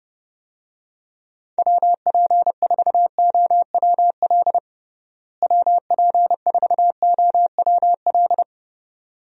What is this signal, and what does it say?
Morse code, heard as a keyed tone, about 32 wpm, 700 Hz. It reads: WP4OWL WP4OWL
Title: USA Call Signs - 6 Characters - Repeated 2x - 30wpm